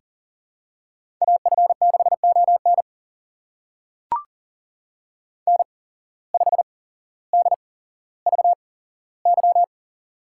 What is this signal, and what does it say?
Morse code, heard as a keyed tone, about 40 wpm, spaced as for 17 wpm, 700 Hz. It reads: AF6ON N5DVY
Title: USA Call Signs - 8x Character Spacing - 40wpm